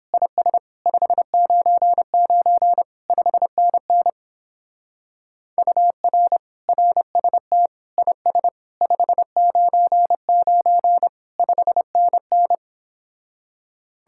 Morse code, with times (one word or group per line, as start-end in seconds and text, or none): 0.1-0.6 IS
0.9-2.8 599
3.1-4.1 5NN
5.6-6.4 UR
6.7-7.7 RST
8.0-8.5 IS
8.8-11.1 599
11.4-12.6 5NN